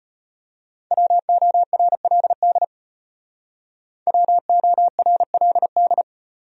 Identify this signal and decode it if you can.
Morse code, heard as a keyed tone, about 33 wpm, 700 Hz. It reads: WORLD WORLD